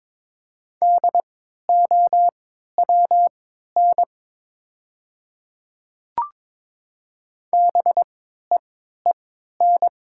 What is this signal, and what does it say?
Morse code, heard as a keyed tone, about 22 wpm, spaced as for 13 wpm, 700 Hz. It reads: DOWN BEEN
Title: 3x Character Spacing - Top 100 Words - 22wpm